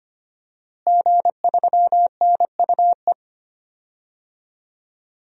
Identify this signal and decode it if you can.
Morse code, heard as a keyed tone, about 25 wpm, 700 Hz. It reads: G3NUE
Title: DXCC-100 Call Signs - 5 Characters - Repeated 3x - 25wpm